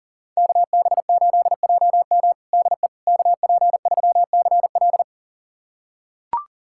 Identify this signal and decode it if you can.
Morse code, heard as a keyed tone, about 40 wpm, 700 Hz. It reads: KB8JM DE KP3CL
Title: USA Callsign DE Callsign - 5 Characters - 40wpm